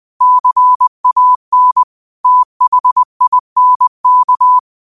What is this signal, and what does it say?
Morse code, heard as a keyed tone, about 20 wpm, 1000 Hz. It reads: CAN THINK